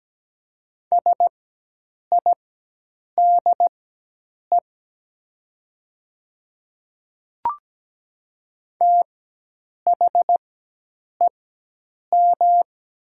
Morse code, heard as a keyed tone, about 17 wpm, spaced as for 8 wpm, 700 Hz. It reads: SIDE THEM
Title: 4x Character Spacing - Top 100 Words - 17wpm